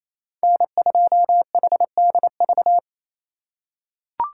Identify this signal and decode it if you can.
Morse code, heard as a keyed tone, about 28 wpm, 700 Hz. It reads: N2HDV
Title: USA Call Signs - 5 Characters - 2x Character Spacing - 28wpm